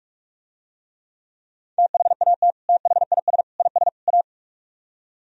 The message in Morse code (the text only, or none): THAT THIS IS A